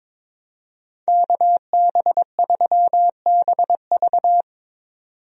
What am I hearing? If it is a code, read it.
Morse code, heard as a keyed tone, about 22 wpm, 700 Hz. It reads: KB3BV
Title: USA Call Signs - 5 Characters - 22wpm